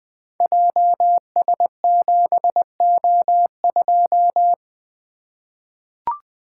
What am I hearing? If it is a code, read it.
Morse code, heard as a keyed tone, about 20 wpm, 700 Hz. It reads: JS7O2
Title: Code Groups - 5 Letter:Numbers - 20wpm